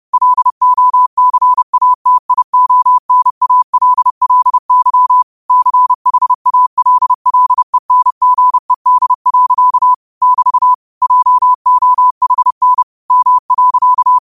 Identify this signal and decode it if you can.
Morse code, heard as a keyed tone, about 30 wpm, 1000 Hz. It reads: LOCATIONALLY CHALLENGED. <BT> JOHN M.
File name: qotdfiles / 30wpm